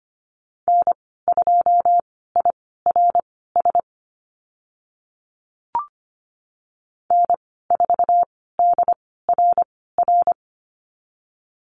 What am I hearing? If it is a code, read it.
Morse code, heard as a keyed tone, about 25 wpm, spaced as for 16 wpm, 700 Hz. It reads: N2IRS N4DRR
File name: Lesson 15g - Number 5 - Call Signs 25wpm